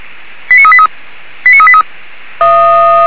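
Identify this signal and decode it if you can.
Five-tone selective call (ZVEI), heard as quick runs of five tones, tones of about 70 ms. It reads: 89383 89383
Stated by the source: ZVEI five-tone call received by SDR, from the feuerwehr-orlamuende.org fire brigade audio archive.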